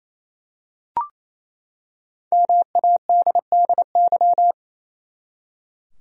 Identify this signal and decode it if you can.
Morse code, heard as a keyed tone, about 28 wpm, 700 Hz. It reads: MADDY
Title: USA Common Names - Max 5 Letters - 28wpm